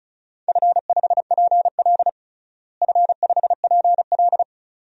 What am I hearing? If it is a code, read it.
Morse code, heard as a keyed tone, about 36 wpm, 700 Hz. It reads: F5PL F5PL
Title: DXCC-100 Call Signs - 4 Characters - Repeated 2x - 35wpm